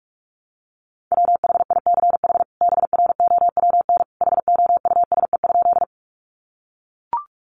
Morse code, heard as a keyed tone, about 44 wpm, 700 Hz. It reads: WHICH BROWN HOUSE?
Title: Sentences from Top 1200 Words 45wpm